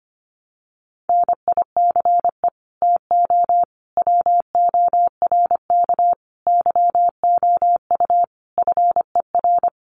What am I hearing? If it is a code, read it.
Morse code, heard as a keyed tone, about 25 wpm, 700 Hz. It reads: NICE TO WORK YOU FER
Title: Common QSO Phrases 25wpm